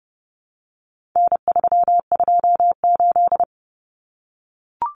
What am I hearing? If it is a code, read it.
Morse code, heard as a keyed tone, about 30 wpm, 700 Hz. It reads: N328
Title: Common Cut Numbers - 4-Digit 30wpm